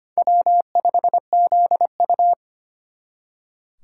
Morse code, heard as a keyed tone, about 33 wpm, 700 Hz. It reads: W5ZU